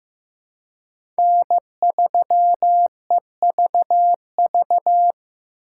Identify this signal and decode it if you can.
Morse code, heard as a keyed tone, about 15 wpm, 700 Hz. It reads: N3EVV
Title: USA Call Signs - 5 Characters - 4x Character Spacing - 15wpm